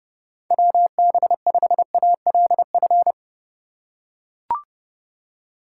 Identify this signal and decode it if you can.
Morse code, heard as a keyed tone, about 32 wpm, 700 Hz. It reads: WB5ALF
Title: USA Call Signs - 4x Character Spacing - 30wpm